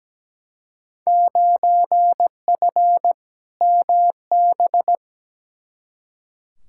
Morse code, heard as a keyed tone, about 17 wpm, 700 Hz. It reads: 9F MB
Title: Field Day Exchanges Encore - 17wpm